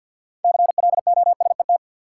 Morse code, heard as a keyed tone, about 47 wpm, 700 Hz. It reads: CLOSET
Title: Six Letter Words - 50wpm